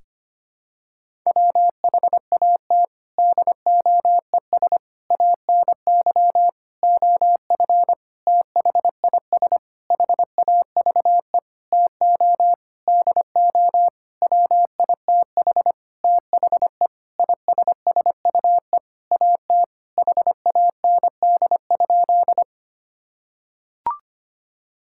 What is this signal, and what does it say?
Morse code, heard as a keyed tone, about 25 wpm, 700 Hz. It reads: WHAT DOES ANY OF THIS HAVE TO DO WITH THE ISSUE AT HAND?
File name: practice / 25wpm / Sentences with 9-Word Phrases - 25wpm